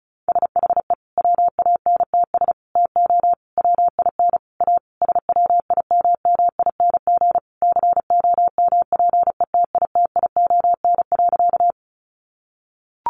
Morse code, heard as a keyed tone, about 34 wpm, 700 Hz. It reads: SHE WANTS TO WIN A SWIMMING COMPETITION.
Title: Additional Easy Phrases 35wpm